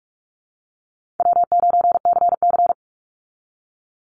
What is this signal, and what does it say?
Morse code, heard as a keyed tone, about 44 wpm, 700 Hz. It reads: W9CC
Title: Call Signs Repeated 2x - 45wpm